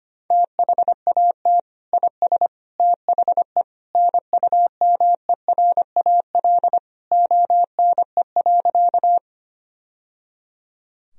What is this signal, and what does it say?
Morse code, heard as a keyed tone, about 25 wpm, 700 Hz. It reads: THAT IS THE NUMERAL ONE.
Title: Sentences from Top 1000 Words - Review Part 2 - 25wpm